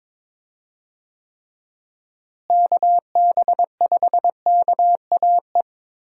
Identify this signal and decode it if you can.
Morse code, heard as a keyed tone, about 22 wpm, 700 Hz. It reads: KB5KAE
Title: USA Call Signs - 6 Characters - Repeated 2x - 22wpm